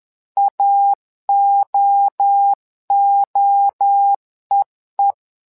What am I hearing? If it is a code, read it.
Morse code, heard as a keyed tone, about 11 wpm, 800 Hz. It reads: AOOEE
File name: LEZIONI / 3 MWRO / 3 MWRO 10wpm 2148